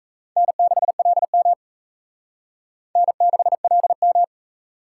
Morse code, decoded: N6LM N6LM

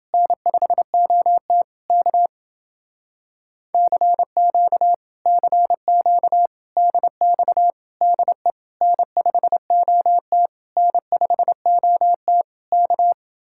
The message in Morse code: N5OT K CQ CQ DX DE N5OT N5OT K

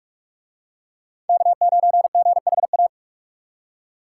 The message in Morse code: K9GHA

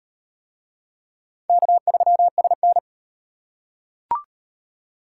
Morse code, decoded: K3SN

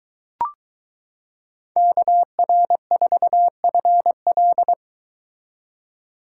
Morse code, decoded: KR4FL

